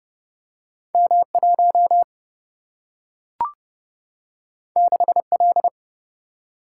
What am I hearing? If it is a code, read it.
Morse code, heard as a keyed tone, about 30 wpm, 700 Hz. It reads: M1 6L
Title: Single-Letter Single-Number 30wpm